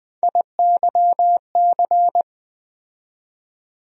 Morse code, heard as a keyed tone, about 20 wpm, 700 Hz. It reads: IYC
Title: USA Call Sign Postfixes - 3 Letters 20wpm